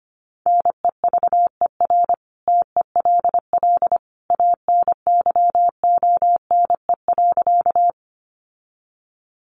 Morse code, heard as a keyed tone, about 25 wpm, 700 Hz. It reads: NEVER TELL ANYONE.
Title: Sentences from Top 900 Words 25wpm